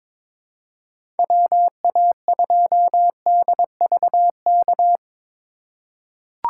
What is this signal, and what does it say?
Morse code, heard as a keyed tone, about 22 wpm, 700 Hz. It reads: WA2DVK
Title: USA Call Signs - 6 Characters - 8x Character Spacing - 22wpm